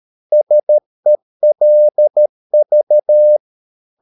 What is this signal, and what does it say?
Morse code, heard as a keyed tone, about 13 wpm, 600 Hz. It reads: SELV
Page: Ferie 13 wpm